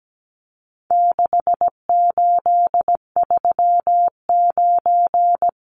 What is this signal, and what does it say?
Morse code, heard as a keyed tone, about 17 wpm, 700 Hz. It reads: 6839
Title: Numbers - 4-Digit 17wpm